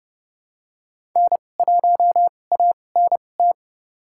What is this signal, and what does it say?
Morse code, heard as a keyed tone, about 30 wpm, spaced as for 22 wpm, 700 Hz. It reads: N1ANT
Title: USA Call Signs - 5 Characters - 2x Character Spacing - 30wpm